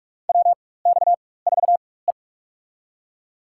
Morse code, heard as a keyed tone, about 46 wpm, spaced as for 21 wpm, 700 Hz. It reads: WX4E
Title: USA Call Signs - 4x Character Spacing - 45wpm